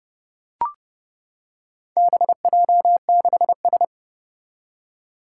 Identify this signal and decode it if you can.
Morse code, heard as a keyed tone, about 30 wpm, 700 Hz. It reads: BJ6S
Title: Four Challenging Letter-Number Combinations 30wpm